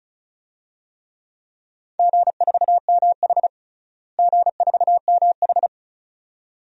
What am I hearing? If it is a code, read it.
Morse code, heard as a keyed tone, about 36 wpm, 700 Hz. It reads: G4MH G4MH
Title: DXCC-100 Call Signs - 4 Characters - Repeated 2x - 35wpm